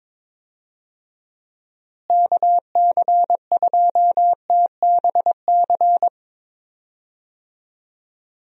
Morse code, heard as a keyed tone, about 22 wpm, 700 Hz. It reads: KC2TBC
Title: USA Call Signs - 6 Characters - Repeated 3x - 22wpm